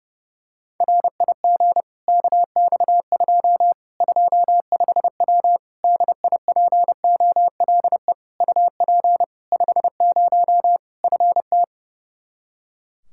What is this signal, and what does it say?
Morse code, heard as a keyed tone, about 30 wpm, 700 Hz. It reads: RIG KX2 25W DIPOLE UP 50 FT
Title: Practice QSO - Exchange 2 - Info - 30wpm